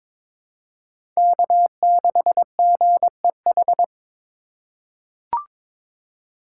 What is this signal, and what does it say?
Morse code, heard as a keyed tone, about 21 wpm, 700 Hz. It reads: K6GEH
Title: USA Call Signs - 3x Character Spacing - 22wpm